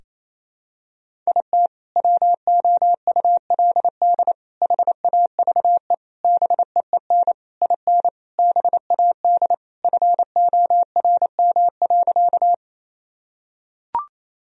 Morse code, heard as a keyed tone, about 28 wpm, 700 Hz. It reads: IT WOULD HAVE BEEN IN BAD FORM.
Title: Sentences from Top 1000 Words - Review Part 1 - 28wpm